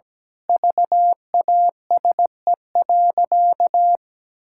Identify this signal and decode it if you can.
Morse code, heard as a keyed tone, about 17 wpm, 700 Hz. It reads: VASE.
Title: Pangram Challenge v2 - 17wpm